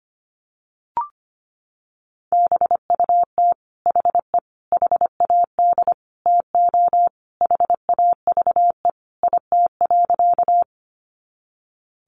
Morse code, heard as a keyed tone, about 25 wpm, 700 Hz. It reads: BUT HE HAD TO HAVE IT.
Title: Sentences from Top 100 Words 25wpm